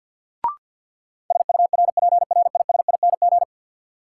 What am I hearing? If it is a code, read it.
Morse code, heard as a keyed tone, about 49 wpm, 700 Hz. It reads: SURPRISING
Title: Sets of 5 Words - New Word Drill - 50wpm